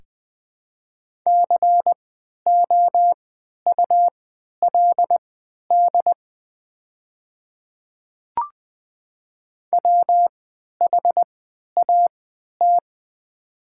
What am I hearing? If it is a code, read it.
Morse code, heard as a keyed tone, about 20 wpm, spaced as for 11 wpm, 700 Hz. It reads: COULD WHAT
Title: 3x Character Spacing - Top 100 Words - 20wpm